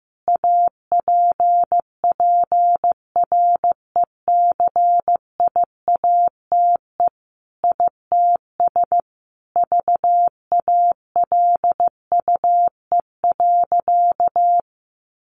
Morse code, heard as a keyed tone, about 15 wpm, 700 Hz. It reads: APPRECIATE ITS VALUE.
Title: Sentences from Top 1300 Words 15wpm